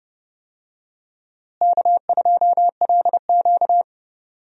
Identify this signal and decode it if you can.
Morse code, heard as a keyed tone, about 32 wpm, 700 Hz. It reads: K2LQ
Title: USA Call Signs - 4 Characters - Repeated 3x - 30wpm